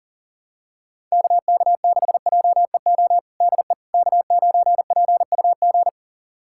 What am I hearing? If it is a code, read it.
Morse code, heard as a keyed tone, about 40 wpm, 700 Hz. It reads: KK6JEO DE K9PUG
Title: USA Callsign DE Callsign 40wpm